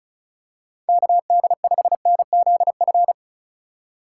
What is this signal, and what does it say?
Morse code, heard as a keyed tone, about 36 wpm, 700 Hz. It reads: KD5NZF
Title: USA Call Signs - 6 Characters - 3x Character Spacing - 35wpm